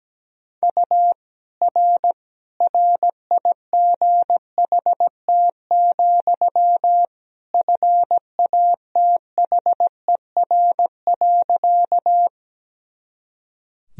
Morse code, heard as a keyed tone, about 17 wpm, 700 Hz. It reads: U R RIGHT, FATHER.
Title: Sentences from Top 300 Words and Abbreviations 17wpm